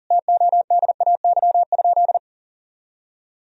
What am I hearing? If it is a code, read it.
Morse code, heard as a keyed tone, about 37 wpm, 700 Hz. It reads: TODAY?
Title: Sentences from Top 1000 Words - Review Part 2 - 40wpm